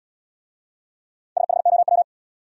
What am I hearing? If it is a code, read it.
Morse code, heard as a keyed tone, about 74 wpm, 700 Hz. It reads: SHOW